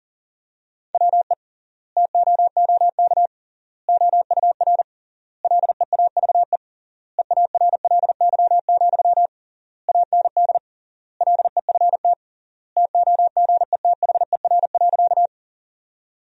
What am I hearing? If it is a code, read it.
Morse code, heard as a keyed tone, about 40 wpm, 700 Hz. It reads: WE TOOK OUR LEAVE EARLY, AND LEFT TOGETHER.
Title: 3x Word-Spacing - Sentences from Top 400 Words - 40wpm